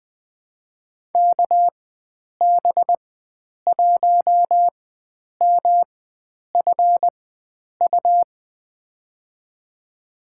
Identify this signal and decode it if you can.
Morse code, heard as a keyed tone, about 20 wpm, spaced as for 9 wpm, 700 Hz. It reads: KB1MFU
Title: USA Call Signs - 6 Characters - 4x Character Spacing - 20wpm